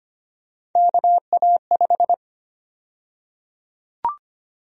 Morse code, heard as a keyed tone, about 25 wpm, 700 Hz. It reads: KA5
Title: USA Call Sign Prefixes - Repeated 2x - 25wpm